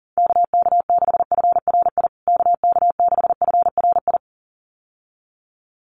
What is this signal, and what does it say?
Morse code, heard as a keyed tone, about 40 wpm, 700 Hz. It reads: KK6FRI KK6FRI
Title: USA Call Signs Repeated 2x 40wpm